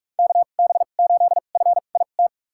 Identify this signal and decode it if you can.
Morse code, heard as a keyed tone, about 43 wpm, spaced as for 32 wpm, 700 Hz. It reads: KB8FIT
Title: USA Call Signs - 6 Characters - 2x Character Spacing - 45wpm